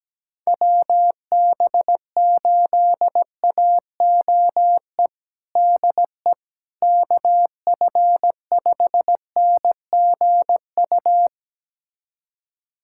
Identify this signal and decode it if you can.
Morse code, heard as a keyed tone, about 17 wpm, 700 Hz. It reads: WB8AOE DE KF5NGU
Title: USA Callsign DE Callsign - 6 Characters - 17wpm